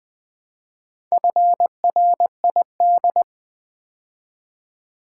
Morse code, heard as a keyed tone, about 20 wpm, 690 Hz. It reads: FRID
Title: Build A Word - 2 Letters At A Time - 6-Letter Words - 20wpm